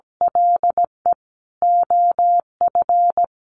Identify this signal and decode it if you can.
Morse code, heard as a keyed tone, about 17 wpm, 700 Hz. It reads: LE OF
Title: Sentences from Top 700 Words 17wpm